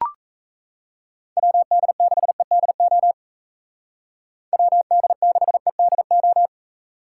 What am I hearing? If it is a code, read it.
Morse code, heard as a keyed tone, about 34 wpm, 700 Hz. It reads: WD6EDO WD6EDO
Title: Speed Racing - USA 2x3 Call Signs 28wpm